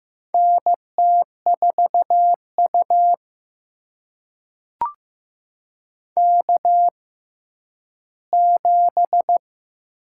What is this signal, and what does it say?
Morse code, heard as a keyed tone, about 15 wpm, spaced as for 10 wpm, 700 Hz. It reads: NT4U K7
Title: USA Call Signs - 4 Characters - 6x Character Spacing - 15wpm